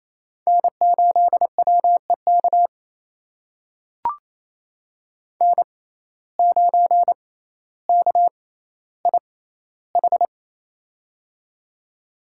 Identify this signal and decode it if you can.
Morse code, heard as a keyed tone, about 28 wpm, spaced as for 14 wpm, 700 Hz. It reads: N8WEK N9KIH
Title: USA Call Signs - 5 Characters - 6x Character Spacing - 28wpm